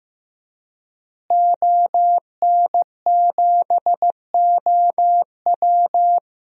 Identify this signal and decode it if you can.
Morse code, heard as a keyed tone, about 16 wpm, 700 Hz. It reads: ON7OW
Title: DXCC-100 Call Signs - 5 Characters - Repeated 3x - 15wpm